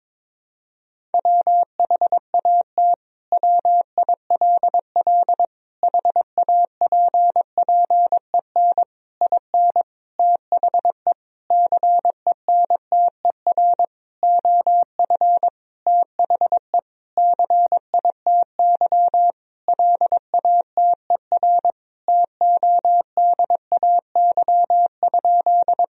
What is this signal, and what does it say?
Morse code, heard as a keyed tone, about 22 wpm, 700 Hz. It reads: WHAT WILL HAPPEN IN THE CENTER OF THE CITY LATER TODAY?
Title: Sentences from Top 1000 Words - Review Part 2 - 22wpm